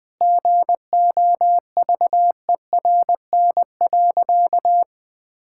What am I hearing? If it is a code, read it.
Morse code, heard as a keyed tone, about 20 wpm, 700 Hz. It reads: GOVERN.
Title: Sentences from Top 1000 Words - Review Part 5 - 20wpm